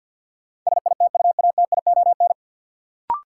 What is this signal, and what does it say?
Morse code, heard as a keyed tone, about 52 wpm, 700 Hz. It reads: SITUATION
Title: Sentences from Top 1200 Words - New Word Drill - 50wpm